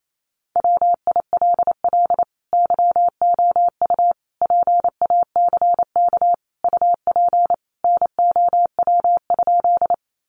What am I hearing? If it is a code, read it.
Morse code, heard as a keyed tone, about 28 wpm, 700 Hz. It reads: WILL YOU PACK UP NOW?
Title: Sentences from Top 1300 Words 28wpm